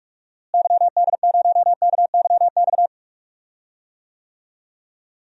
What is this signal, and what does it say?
Morse code, heard as a keyed tone, about 41 wpm, 700 Hz. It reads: YD0KYX
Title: DXCC-100 Call Signs - 6 Characters - Repeated 2x - 45wpm